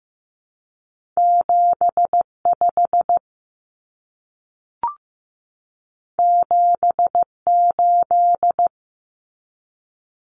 Morse code, heard as a keyed tone, about 15 wpm, 700 Hz. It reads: 75 78
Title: Common Cut Numbers - 2-Digit 15wpm